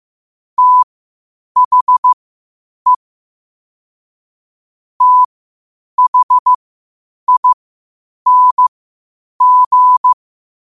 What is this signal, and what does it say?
Morse code, heard as a keyed tone, about 15 wpm, spaced as for 8 wpm, 1000 Hz. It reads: THE THING
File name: qotdfiles / 8wpm